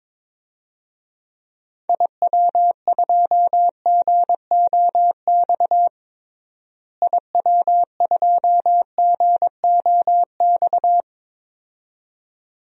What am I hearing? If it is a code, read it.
Morse code, heard as a keyed tone, about 22 wpm, 700 Hz. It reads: IW2GOX IW2GOX
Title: DXCC-100 Call Signs - 6 Characters - Repeated 2x - 22wpm